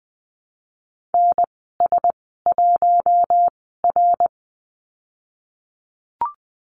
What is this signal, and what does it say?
Morse code, heard as a keyed tone, about 20 wpm, spaced as for 14 wpm, 700 Hz. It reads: NS1R
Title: 055 Lesson 10g - Letter r - Call Signs 20wpm